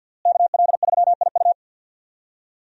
Morse code, heard as a keyed tone, about 52 wpm, 700 Hz. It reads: KL3IU
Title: USA Call Signs - 5 Characters - Repeated 3x - 50wpm